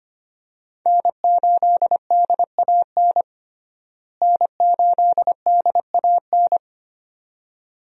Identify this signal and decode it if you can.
Morse code, heard as a keyed tone, about 25 wpm, 700 Hz. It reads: N8DAN N8DAN
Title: USA Call Signs - 5 Characters - Repeated 2x - 25wpm